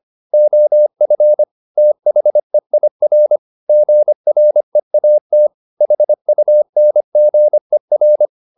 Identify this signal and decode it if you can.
Morse code, heard as a keyed tone, about 25 wpm, 600 Hz. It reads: OF THEIR GREAT HUNGER